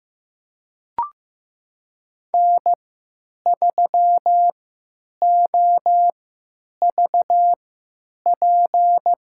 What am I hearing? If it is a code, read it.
Morse code, heard as a keyed tone, about 15 wpm, spaced as for 9 wpm, 700 Hz. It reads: N3OVP
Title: USA Call Signs - 5 Characters - 3x Character Spacing - 15wpm